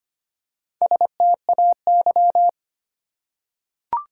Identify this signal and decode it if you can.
Morse code, heard as a keyed tone, about 25 wpm, 700 Hz. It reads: STAY
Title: Four Letter Words - 25wpm